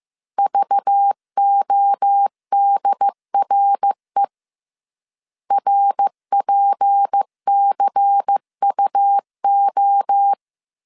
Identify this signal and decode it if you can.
Morse code, heard as a keyed tone, about 15 wpm, 800 Hz. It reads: VODRE RPCUO
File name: CW / DET / 1 6 1500C S C 14wpm 2951